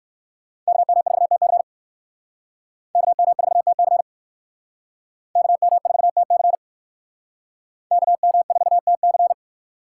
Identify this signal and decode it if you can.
Morse code, heard as a keyed tone, about 59 wpm, 700 Hz. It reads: KM4TC KM4TC KM4TC KM4TC